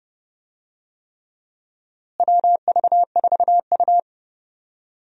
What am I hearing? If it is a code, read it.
Morse code, heard as a keyed tone, about 30 wpm, 700 Hz. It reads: WV4U
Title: USA Call Signs - 4 Characters - Repeated 3x - 30wpm